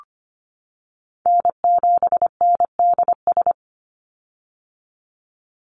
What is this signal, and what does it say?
Morse code, heard as a keyed tone, about 25 wpm, 700 Hz. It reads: N7NDS
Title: USA Call Signs 25wpm